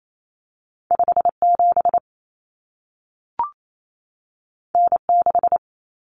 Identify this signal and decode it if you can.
Morse code, heard as a keyed tone, about 28 wpm, 700 Hz. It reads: 57 N6
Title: Common Cut Numbers - 2-Digit 28wpm